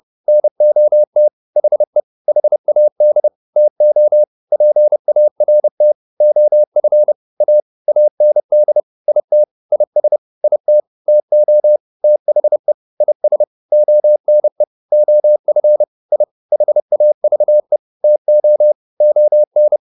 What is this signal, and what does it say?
Morse code, heard as a keyed tone, about 30 wpm, 600 Hz. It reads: NOT HE HAD TO PART OF A AND IT IS IT TO THE IS ONE OF I HAVE TO ON